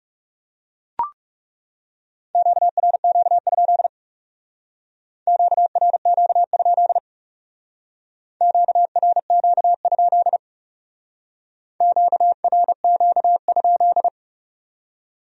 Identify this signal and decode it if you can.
Morse code, heard as a keyed tone, about 38 wpm, 700 Hz. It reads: QRQ? QRQ? QRQ? QRQ?